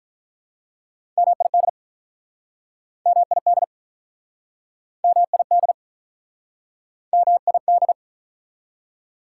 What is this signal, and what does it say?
Morse code, heard as a keyed tone, about 48 wpm, 700 Hz. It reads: MID MID MID MID